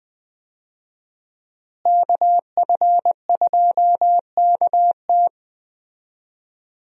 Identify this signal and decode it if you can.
Morse code, heard as a keyed tone, about 20 wpm, 700 Hz. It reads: KF2KT
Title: USA Call Signs - 5 Characters - Repeated 2x - 20wpm